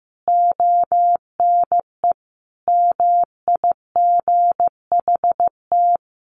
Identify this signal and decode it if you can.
Morse code, heard as a keyed tone, about 15 wpm, 700 Hz. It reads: ONE MIGHT
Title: Sentences from Top 1500 Words 15wpm